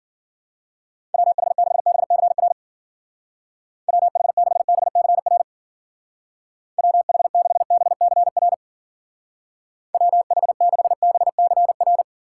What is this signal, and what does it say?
Morse code, heard as a keyed tone, about 50 wpm, 700 Hz. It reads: WH6BCR WH6BCR WH6BCR WH6BCR
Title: Speed Racing - USA 2x3 Call Signs 40wpm